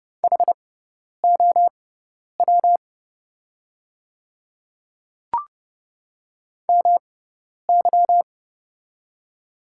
Morse code, heard as a keyed tone, about 30 wpm, spaced as for 10 wpm, 700 Hz. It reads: HOW MY
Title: 6x Character Spacing - Top 100 Words - 30wpm